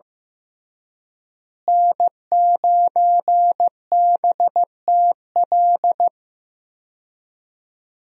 Morse code, heard as a keyed tone, about 15 wpm, 700 Hz. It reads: N9BTL
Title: USA Call Signs - 5 Characters - Repeated 3x - 15wpm